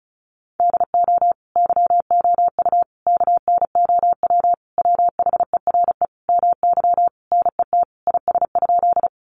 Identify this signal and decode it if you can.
Morse code, heard as a keyed tone, about 34 wpm, 700 Hz. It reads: DO YOU KNOW WHERE MY NET IS?
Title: Sentences from Top 1400 Words 35wpm